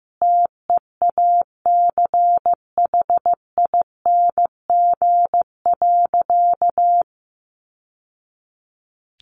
Morse code, heard as a keyed tone, about 14 wpm, 700 Hz. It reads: TEACHING.
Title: Sentences from Top 1100 Words 15wpm